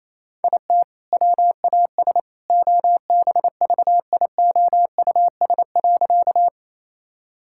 Morse code, heard as a keyed tone, about 27 wpm, 700 Hz. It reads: IT WAS OBVIOUS.
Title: Sentences from Top 1600 Words 28wpm